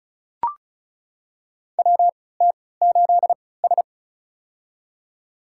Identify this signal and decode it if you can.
Morse code, heard as a keyed tone, about 36 wpm, spaced as for 20 wpm, 700 Hz. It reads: WT8S
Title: USA Call Signs - 4 Characters - 3x Character Spacing - 35wpm